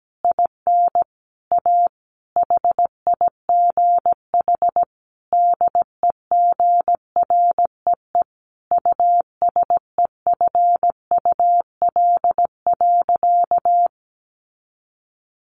Morse code, heard as a keyed tone, about 17 wpm, 700 Hz. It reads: IN A HIGH DEGREE USEFUL.
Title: Sentences from Top 1500 Words 17wpm